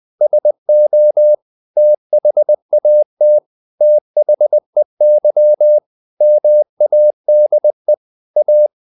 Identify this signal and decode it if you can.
Morse code, heard as a keyed tone, about 20 wpm, 600 Hz. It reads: SO THAT THEY MADE A
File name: the old man and the sea / the old man and the sea part 1 20wpm